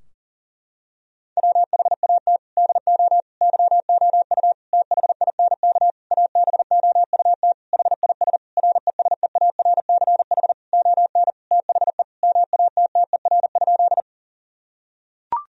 Morse code, heard as a keyed tone, about 37 wpm, 700 Hz. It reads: WHAT DO YOU THINK ABOUT HIS RESEARCH ON THE MATTER?
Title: Sentences from Top 1000 Words - Review Part 4 - 40wpm